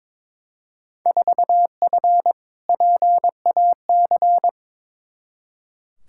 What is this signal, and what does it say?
Morse code, heard as a keyed tone, about 21 wpm, 700 Hz. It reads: 4F PAC
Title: Field Day Exchanges Encore - 22wpm